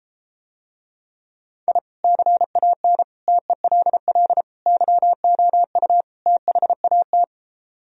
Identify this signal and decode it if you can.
Morse code, heard as a keyed tone, about 33 wpm, 700 Hz. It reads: I CAN TELL YOU THAT